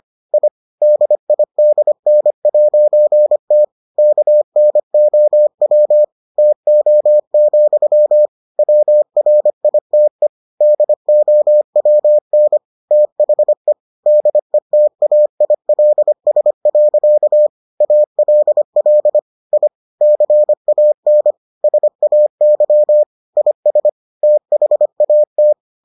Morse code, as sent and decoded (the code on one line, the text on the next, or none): .. / -.. .. -.. -. .----. - / -.- -. --- .-- / - --- --..-- / .-- .-. .. - . / -.. --- .-- -. / - .... / -.. - .- .. .-.. ... .-.-.- / .- .-.. .-.. / .. / -.-. .- -. / ... .- -.-- / .. ... / - .... .- -
I DIDN'T KNOW TO, WRITE DOWN THE DETAILS. ALL I CAN SAY IS THAT